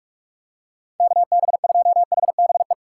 Morse code, heard as a keyed tone, about 44 wpm, 700 Hz. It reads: KB2HBE
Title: USA Call Signs - 6 Characters - 4x Character Spacing - 45wpm